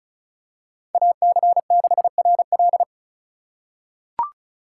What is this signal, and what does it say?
Morse code, heard as a keyed tone, about 36 wpm, 700 Hz. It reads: AC6RL
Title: USA Call Signs - 5 Characters - 2x Character Spacing - 35wpm